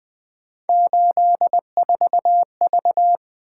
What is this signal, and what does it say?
Morse code, heard as a keyed tone, about 23 wpm, 700 Hz. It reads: 84V
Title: Speed-Racing - Three Letter-Number Combinations - 20wpm